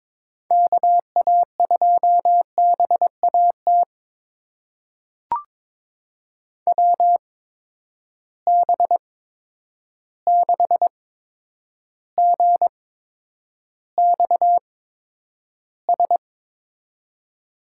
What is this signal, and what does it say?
Morse code, heard as a keyed tone, about 22 wpm, spaced as for 9 wpm, 700 Hz. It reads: KA2BAT WB6GXS
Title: USA Call Signs - 6 Characters - 8x Character Spacing - 22wpm